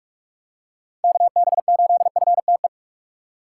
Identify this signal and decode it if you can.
Morse code, heard as a keyed tone, about 46 wpm, 700 Hz. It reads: KB8FTE
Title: USA Call Signs - 3x Character Spacing - 45wpm